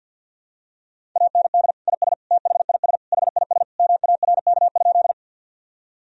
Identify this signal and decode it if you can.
Morse code, heard as a keyed tone, about 50 wpm, 700 Hz. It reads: AND IS THIS HIS MARK?
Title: Sentences from Top 1000 Words - Review Part 1 - 50wpm